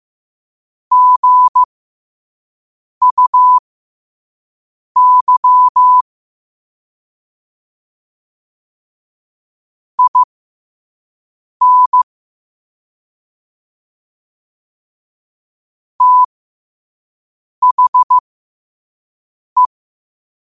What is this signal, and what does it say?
Morse code, heard as a keyed tone, about 15 wpm, spaced as for 5 wpm, 1000 Hz. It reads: GUY IN THE
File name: qotdfiles / 5wpm